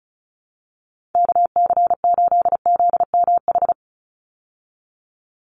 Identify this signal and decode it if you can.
Morse code, heard as a keyed tone, about 34 wpm, 700 Hz. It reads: KC8ZMH
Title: USA Call Signs 35wpm